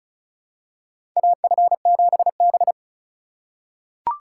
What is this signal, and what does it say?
Morse code, heard as a keyed tone, about 36 wpm, 700 Hz. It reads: AF7B
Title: USA Call Signs - 4 Characters - 3x Character Spacing - 35wpm